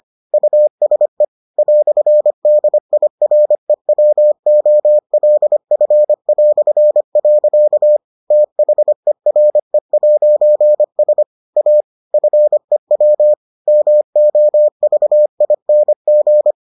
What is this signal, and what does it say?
Morse code, heard as a keyed tone, about 25 wpm, 600 Hz. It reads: USE "DIREWOLF". THERE'S A FEW MOVING